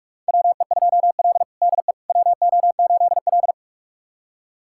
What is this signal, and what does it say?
Morse code, heard as a keyed tone, about 42 wpm, 700 Hz. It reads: WE2L DE WO8L
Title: USA Callsign DE Callsign - 4 Characters - 45wpm